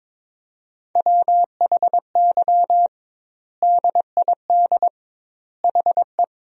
Morse code, heard as a keyed tone, about 22 wpm, 700 Hz. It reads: WHY DID HE
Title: 2x Word-Spacing - Sentences from Top 200 Words - 22wpm